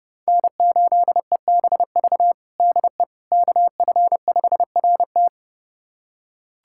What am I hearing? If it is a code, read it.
Morse code, heard as a keyed tone, about 30 wpm, 700 Hz. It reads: N8EBV DE KF5RT
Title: USA Callsign DE Callsign - 5 Characters - 30wpm